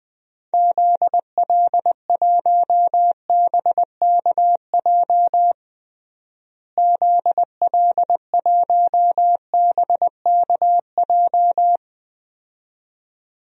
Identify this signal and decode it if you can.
Morse code, heard as a keyed tone, about 20 wpm, 700 Hz. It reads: ZL1BKJ ZL1BKJ
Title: DXCC-100 Call Signs - 6 Characters - Repeated 2x - 20wpm